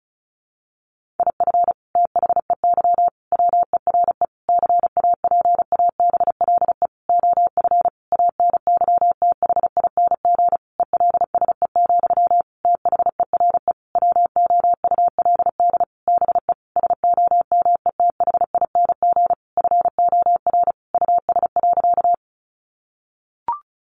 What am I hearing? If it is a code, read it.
Morse code, heard as a keyed tone, about 34 wpm, 700 Hz. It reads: IF THEY WERE CAPABLE OF ANYTHING ELSE, THERE WOULD BE SOMETHING FOR US.
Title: Sentences from Top 1200 Words 35wpm